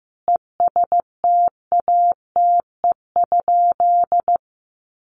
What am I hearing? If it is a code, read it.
Morse code, heard as a keyed tone, about 15 wpm, 700 Hz. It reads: ESTATE?
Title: Sentences from Top 1200 Words 15wpm